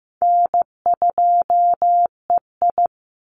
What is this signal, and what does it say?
Morse code, heard as a keyed tone, about 15 wpm, 700 Hz. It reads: N2EI
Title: Call Signs Repeated 3x - 15wpm